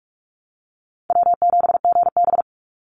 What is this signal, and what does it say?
Morse code, heard as a keyed tone, about 44 wpm, 700 Hz. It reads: W7GB
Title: Easy Call Signs Repeated 2x - 45wpm